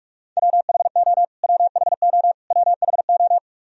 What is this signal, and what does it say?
Morse code, heard as a keyed tone, about 46 wpm, 700 Hz. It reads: WHO WHO WHO
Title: Top 100 Words Repeated 45wpm